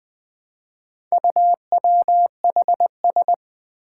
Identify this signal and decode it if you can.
Morse code, heard as a keyed tone, about 20 wpm, 700 Hz. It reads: UWHS
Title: Four Challenging Letter-Number Combinations 20wpm